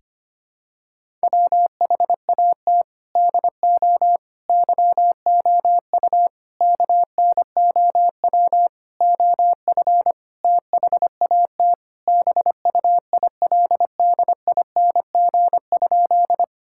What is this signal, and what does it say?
Morse code, heard as a keyed tone, about 25 wpm, 700 Hz. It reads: WHAT DO YOU KNOW OF THAT BUILDING?
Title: Sentences from Top 1000 Words - Review Part 5 - 25wpm